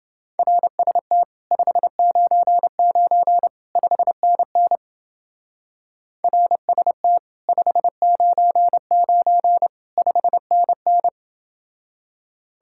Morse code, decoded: RST 599 5NN RST 599 5NN

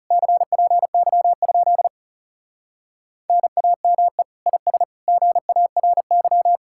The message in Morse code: CPY? NAME IS GARY

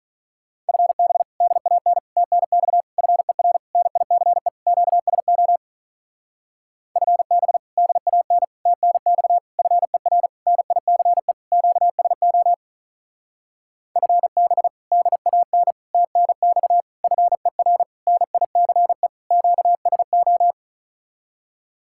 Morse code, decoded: FB DAN TNX FER NICE QSO FB DAN TNX FER NICE QSO FB DAN TNX FER NICE QSO

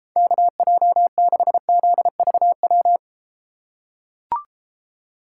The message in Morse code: KJ6ZVW

